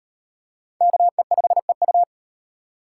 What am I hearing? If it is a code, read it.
Morse code, heard as a keyed tone, about 38 wpm, 700 Hz. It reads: KE5EU